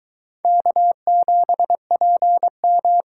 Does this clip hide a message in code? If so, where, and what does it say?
Yes: Morse code from 0.5 to 3.0 s, heard as a keyed tone, about 23 wpm, 700 Hz. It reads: K7PM